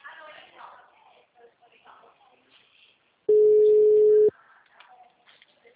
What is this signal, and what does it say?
Ringback tone (Europe, 425 Hz), with speech.
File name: out- -2041- - - .24704